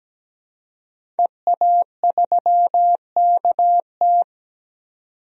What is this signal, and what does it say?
Morse code, heard as a keyed tone, about 17 wpm, 700 Hz. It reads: EA3KT
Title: DXCC-100 Call Signs - 5 Characters - 17wpm